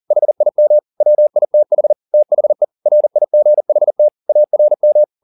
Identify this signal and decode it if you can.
Morse code, heard as a keyed tone, about 40 wpm, 600 Hz. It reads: HIM WITH THE RIGHT ARM